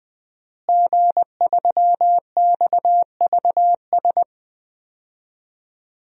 Morse code, heard as a keyed tone, about 21 wpm, 700 Hz. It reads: G3XVS
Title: DXCC-100 Call Signs - 5 Characters - Repeated 2x - 20wpm